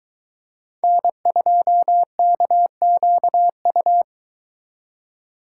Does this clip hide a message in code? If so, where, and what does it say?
Yes: Morse code from 0.8 to 4.0 s, heard as a keyed tone, about 23 wpm, 700 Hz. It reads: N2KQU